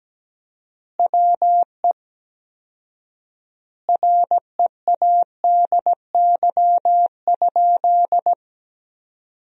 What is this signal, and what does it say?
Morse code, heard as a keyed tone, about 17 wpm, 700 Hz. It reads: WE READY?
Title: 4x Word-Spacing - Sentences from Top 400 Words - 17wpm